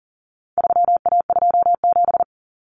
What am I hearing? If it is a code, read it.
Morse code, heard as a keyed tone, about 40 wpm, 700 Hz. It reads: 3A27
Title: Common Cut Numbers - 4-Digit 40wpm